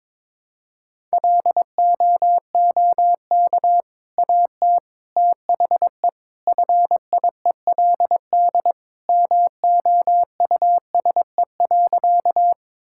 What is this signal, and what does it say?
Morse code, heard as a keyed tone, about 22 wpm, 700 Hz. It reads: LOOK AT THE FIELD MOUSE.
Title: Sentences from Top 1000 Words - Review Part 5 - 22wpm